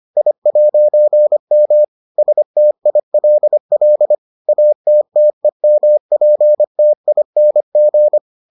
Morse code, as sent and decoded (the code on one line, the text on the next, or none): .. .----. -- / ... - .. .-.. .-.. / .- - - . -- .--. - .. -. --.
I'M STILL ATTEMPTING